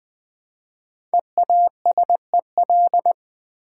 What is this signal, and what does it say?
Morse code, heard as a keyed tone, about 20 wpm, 700 Hz. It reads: EASEL
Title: Easy Words with Hints - 20wpm